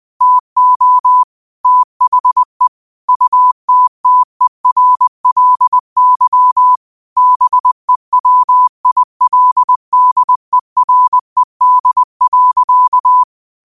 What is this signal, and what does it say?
Morse code, heard as a keyed tone, about 20 wpm, 1000 Hz. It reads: TO THE UTTERLY BEWILDERED.